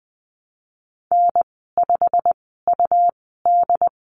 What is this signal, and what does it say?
Morse code, heard as a keyed tone, about 19 wpm, spaced as for 14 wpm, 700 Hz. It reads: N5UD
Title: 122 Lesson 20g - Number 3 - Call Signs 20wpm